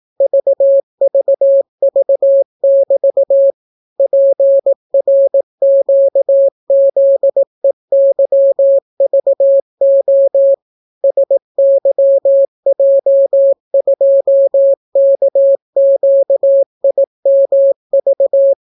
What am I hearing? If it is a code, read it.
Morse code, heard as a keyed tone, about 18 wpm, 550 Hz. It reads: VVV<BT> PRQZEYVO SYJ2KQIMV